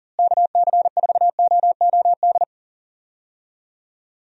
Morse code, heard as a keyed tone, about 40 wpm, 700 Hz. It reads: KC4OOD
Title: USA Call Signs - 6 Characters - Repeated 3x - 40wpm